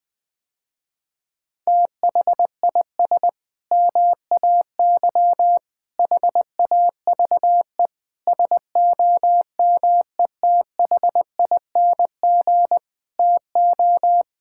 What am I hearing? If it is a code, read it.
Morse code, heard as a keyed tone, about 20 wpm, 700 Hz. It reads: THIS MAY HAVE SOMETHING TO
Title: Sentences with 8-Word Phrases - Spoken Beginning - 20wpm